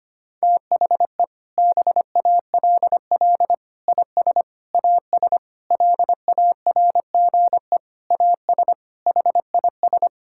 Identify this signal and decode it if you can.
Morse code, heard as a keyed tone, about 25 wpm, 700 Hz. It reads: THE BALL IS AS LARGE AS HIS
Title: Sentences from Top 1000 Words - Review Part 6 - 25wpm